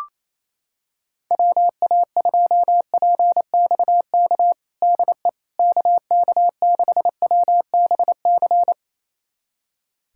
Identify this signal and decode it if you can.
Morse code, heard as a keyed tone, about 28 wpm, 700 Hz. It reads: WA2PXK DE KK6WBC
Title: USA Callsign DE Callsign - 6 Characters - 28wpm